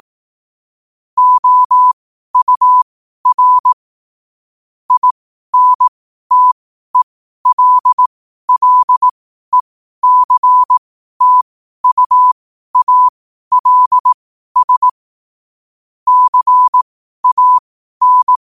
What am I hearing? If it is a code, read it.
Morse code, heard as a keyed tone, about 18 wpm, spaced as for 12 wpm, 1000 Hz. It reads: OUR INTELLECTUALS CAN
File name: qotdfiles / 12wpm